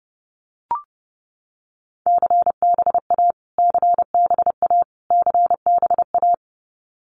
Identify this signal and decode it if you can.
Morse code, heard as a keyed tone, about 30 wpm, 700 Hz. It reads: CBA CBA CBA
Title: Abbreviations Repeated 3x - 30wpm